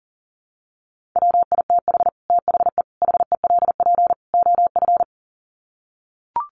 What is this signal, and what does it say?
Morse code, heard as a keyed tone, about 40 wpm, 700 Hz. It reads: WITH THE HELP OF
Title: Sets of 4 Words 40wpm